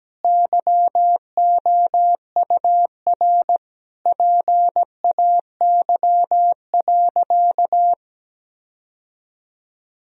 Morse code, heard as a keyed tone, about 17 wpm, 700 Hz. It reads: YOUR PAY.
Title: Sentences from Top 1000 Words - Review Part 4 - 17wpm